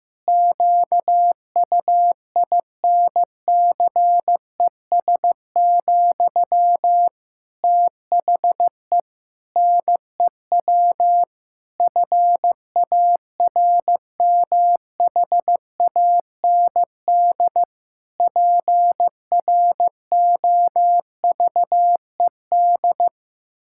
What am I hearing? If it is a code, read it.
Morse code, heard as a keyed tone, about 15 wpm, 700 Hz. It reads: QUINCES, THE NEW FARMHAND PROVED
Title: Pangram Challenge v2 - 15wpm